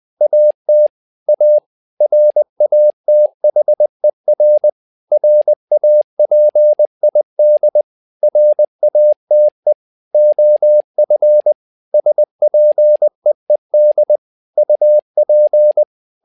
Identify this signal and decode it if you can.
Morse code, heard as a keyed tone, about 20 wpm, 600 Hz. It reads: AT A RATHER RAPID RATE OF SPEED UP
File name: learning center / tower trouble / tower trouble 20wpm